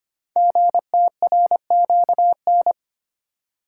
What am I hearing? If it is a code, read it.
Morse code, heard as a keyed tone, about 25 wpm, 700 Hz. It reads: GTRQN
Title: Code Groups - 5 Letters - 25wpm